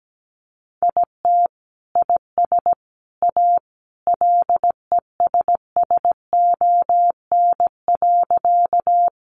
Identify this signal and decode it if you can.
Morse code, heard as a keyed tone, about 17 wpm, 700 Hz. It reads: IT IS A LESSON.
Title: Sentences from Top 1200 Words 17wpm